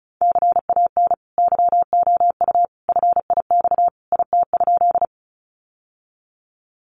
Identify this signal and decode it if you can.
Morse code, heard as a keyed tone, about 34 wpm, 700 Hz. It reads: CAN YOU FIX IT?
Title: Sentences from Top 1400 Words 35wpm